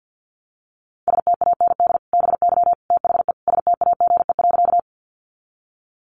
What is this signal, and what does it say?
Morse code, heard as a keyed tone, about 50 wpm, 700 Hz. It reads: STAND BY THE STAGE.
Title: Sentences from Top 1200 Words 50wpm